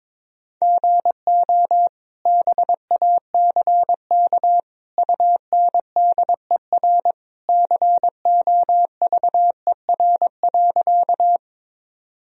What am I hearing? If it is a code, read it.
Morse code, heard as a keyed tone, about 22 wpm, 700 Hz. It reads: GO BACK UNDER COVER.
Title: Sentences from Top 1000 Words - Review Part 1 - 22wpm